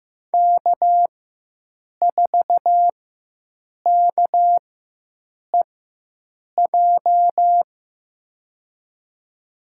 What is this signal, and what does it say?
Morse code, heard as a keyed tone, about 15 wpm, spaced as for 7 wpm, 700 Hz. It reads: K4KEJ
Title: USA Call Signs - 4x Character Spacing - 15wpm